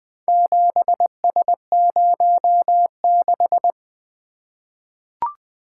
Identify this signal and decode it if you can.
Morse code, heard as a keyed tone, about 20 wpm, 700 Hz. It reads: 7S06
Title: Four Letter-Number Combinations - 20wpm